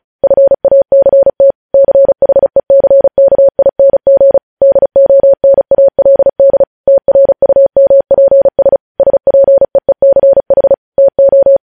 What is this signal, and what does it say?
Morse code, heard as a keyed tone, about 36 wpm, 550 Hz. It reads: FACT CHECKING DONALD TRUMPS SPEECH TO
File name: podcast / News headlines-1x-35x35